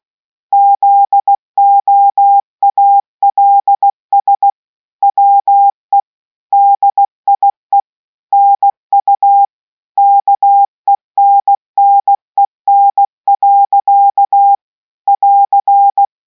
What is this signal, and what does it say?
Morse code, heard as a keyed tone, about 16 wpm, 800 Hz. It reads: ZOALS WE DIE NU KENNEN. <AR>